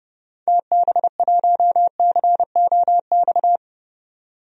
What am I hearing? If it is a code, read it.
Morse code, heard as a keyed tone, about 32 wpm, 700 Hz. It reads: TB1COX
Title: DXCC-100 Call Signs - 6 Characters - 30wpm